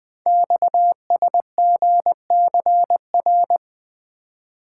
Morse code, heard as a keyed tone, about 20 wpm, 700 Hz. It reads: XSGCR
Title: Code Groups - 5 Letters - 20wpm